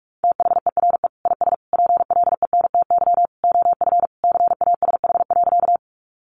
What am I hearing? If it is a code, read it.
Morse code, heard as a keyed tone, about 44 wpm, 700 Hz. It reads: THERE IS PLENTY OF CASH.
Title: Sentences from Top 1300 Words 45wpm